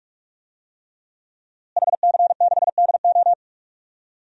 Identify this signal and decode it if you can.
Morse code, heard as a keyed tone, about 45 wpm, 700 Hz. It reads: HC6DO
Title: DXCC-100 Call Signs - 5 Characters - Repeated 3x - 45wpm